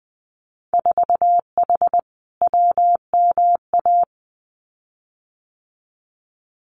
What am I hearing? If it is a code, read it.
Morse code, heard as a keyed tone, about 20 wpm, 700 Hz. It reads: 4H WMA
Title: Winter Field Day Exchanges Encore 20wpm